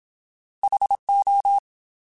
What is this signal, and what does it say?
Morse code, heard as a keyed tone, about 27 wpm, 800 Hz. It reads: HO
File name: Alphabetical / Alphabetical mp3s / Hs @25